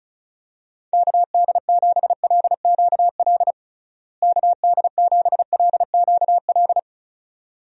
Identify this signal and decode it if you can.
Morse code, heard as a keyed tone, about 36 wpm, 700 Hz. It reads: KD7LQL KD7LQL
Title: USA Call Signs - 6 Characters - Repeated 2x - 35wpm